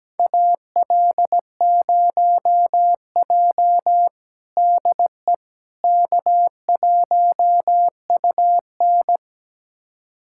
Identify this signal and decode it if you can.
Morse code, heard as a keyed tone, about 17 wpm, 700 Hz. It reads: AL0J DE K1UN
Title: USA Callsign DE Callsign - 4 Characters - 17wpm